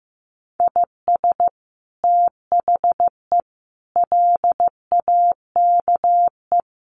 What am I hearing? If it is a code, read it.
Morse code, heard as a keyed tone, about 15 wpm, 700 Hz. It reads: IS THE LAKE
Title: Sentences from Top 1400 Words 15wpm